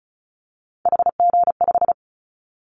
Morse code, heard as a keyed tone, about 36 wpm, 700 Hz. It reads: HG5
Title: Callsign Prefixes for Top 50 DXCC - 35wpm